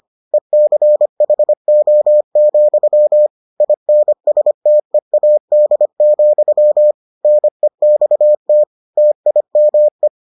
Morse code, heard as a keyed tone, about 25 wpm, 600 Hz. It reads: ECHO, INSTEAD, NEXT TIME